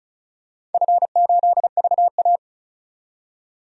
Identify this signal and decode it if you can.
Morse code, heard as a keyed tone, about 36 wpm, 700 Hz. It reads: F8VA
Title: DXCC-100 Call Signs - 4 Characters - 35wpm